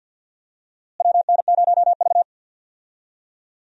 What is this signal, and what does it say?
Morse code, heard as a keyed tone, about 52 wpm, 700 Hz. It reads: WN0V
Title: USA Call Signs - 4 Characters - Repeated 2x - 50wpm